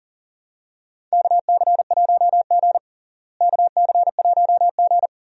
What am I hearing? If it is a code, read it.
Morse code, heard as a keyed tone, about 37 wpm, 700 Hz. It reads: KC1G KC1G
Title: USA Call Signs - 4 Characters - Repeated 3x - 40wpm